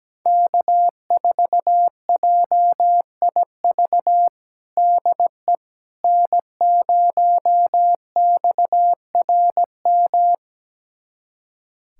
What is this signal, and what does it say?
Morse code, heard as a keyed tone, about 17 wpm, 700 Hz. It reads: K4JIV DE N0XRM
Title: USA Callsign DE Callsign - 5 Characters - 17wpm